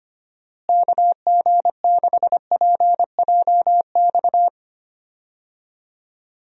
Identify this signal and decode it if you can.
Morse code, heard as a keyed tone, about 25 wpm, 700 Hz. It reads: KG6PJX
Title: USA Call Signs - 6 Characters - Repeated 2x - 25wpm